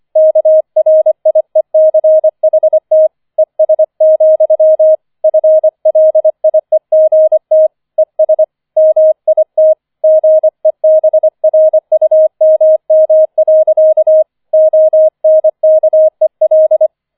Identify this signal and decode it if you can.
Morse code, heard as a keyed tone, about 25 wpm, 630 Hz. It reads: KRIECHT ES, FLIEGT ES MIT GEBRUMM. ONKEL